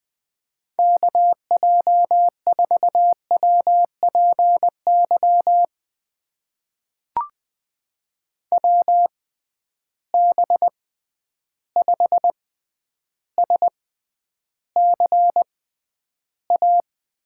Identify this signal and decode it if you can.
Morse code, heard as a keyed tone, about 20 wpm, spaced as for 10 wpm, 700 Hz. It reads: KJ4WPY WB5SCA
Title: USA Call Signs - 6 Characters - 6x Character Spacing - 20wpm